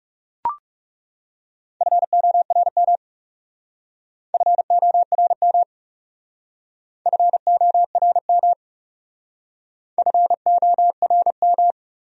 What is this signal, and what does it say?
Morse code, heard as a keyed tone, about 38 wpm, 700 Hz. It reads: FORM FORM FORM FORM